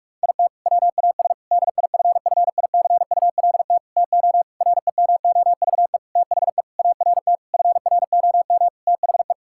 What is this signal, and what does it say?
Morse code, heard as a keyed tone, about 46 wpm, 700 Hz. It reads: IT WAS DIFFICULT TO REMOVE THE ART FROM THE
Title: Sentences from Top 1000 Words - Review Part 6 - 45wpm